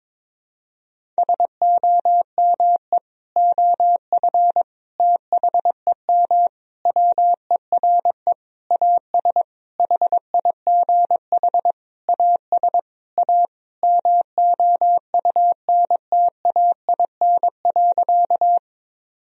Morse code, heard as a keyed tone, about 22 wpm, 700 Hz. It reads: SOME OF THEM WERE AS HIGH AS A MOUNTAIN.
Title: Sentences from Top 1000 Words - Review Part 1 - 22wpm